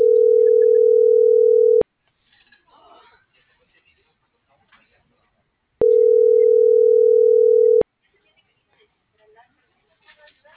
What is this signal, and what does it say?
Ringback tone (North America), 440 + 480 Hz.